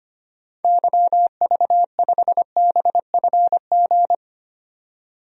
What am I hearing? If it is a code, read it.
Morse code, heard as a keyed tone, about 25 wpm, 700 Hz. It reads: YV5BFG
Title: DXCC-100 Call Signs - 6 Characters - 25wpm